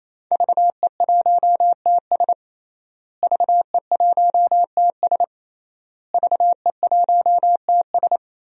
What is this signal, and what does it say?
Morse code, heard as a keyed tone, about 27 wpm, 700 Hz. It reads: VE1TS VE1TS VE1TS
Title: DXCC-100 Call Signs - 5 Characters - Repeated 3x - 28wpm